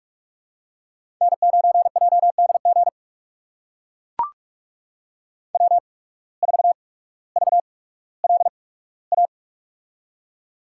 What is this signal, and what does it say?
Morse code, heard as a keyed tone, about 44 wpm, spaced as for 19 wpm, 700 Hz. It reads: N9JDG W4VLA
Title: USA Call Signs - 5 Characters - 8x Character Spacing - 45wpm